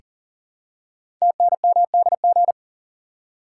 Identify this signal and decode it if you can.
Morse code, heard as a keyed tone, about 40 wpm, 700 Hz. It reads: TNMDG
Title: Code Groups - 5 Letters - 40wpm